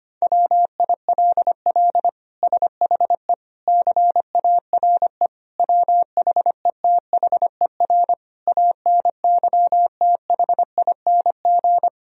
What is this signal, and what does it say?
Morse code, heard as a keyed tone, about 25 wpm, 700 Hz. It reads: WILL SHE CARE WHETHER ANYTHING
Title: Sentences from Top 1000 Words - Review Part 3 - 25wpm